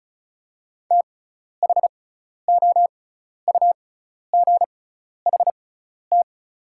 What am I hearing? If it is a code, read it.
Morse code, heard as a keyed tone, about 36 wpm, spaced as for 12 wpm, 690 Hz. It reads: THOUGHT
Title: 6x Character Spacing - Top 300 Words - 35wpm